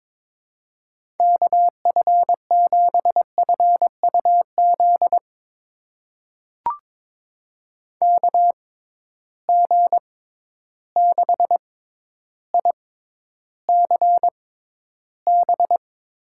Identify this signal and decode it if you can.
Morse code, heard as a keyed tone, about 22 wpm, spaced as for 11 wpm, 700 Hz. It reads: KF7FUZ KG6ICB
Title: USA Call Signs - 6 Characters - 6x Character Spacing - 22wpm